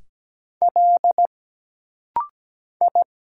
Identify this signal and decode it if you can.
Morse code, heard as a keyed tone, about 17 wpm, 700 Hz. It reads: L I
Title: Single Letters - Rapid-Fire 17wpm